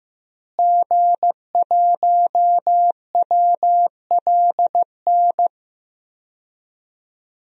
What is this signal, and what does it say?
Morse code, heard as a keyed tone, about 15 wpm, 700 Hz. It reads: G1WLN
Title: DXCC-100 Call Signs - 5 Characters - Repeated 3x - 15wpm